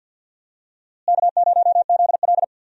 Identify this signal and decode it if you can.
Morse code, heard as a keyed tone, about 51 wpm, 700 Hz. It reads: K0ZL
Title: USA Call Signs - 4 Characters - Repeated 2x - 50wpm